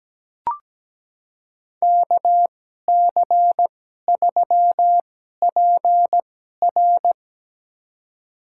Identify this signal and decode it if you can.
Morse code, heard as a keyed tone, about 17 wpm, spaced as for 12 wpm, 700 Hz. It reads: KC3PR
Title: USA Call Signs - 2x Character Spacing - 17wpm